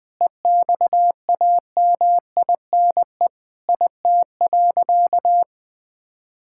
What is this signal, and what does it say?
Morse code, heard as a keyed tone, about 21 wpm, 700 Hz. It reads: EXAMINE IT.
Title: Sentences from Top 1600 Words 20wpm